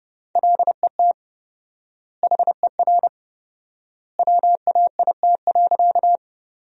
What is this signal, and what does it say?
Morse code, heard as a keyed tone, about 29 wpm, 700 Hz. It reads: LET HER WAIT.
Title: 4x Word-Spacing - Sentences from Top 500 Words - 30wpm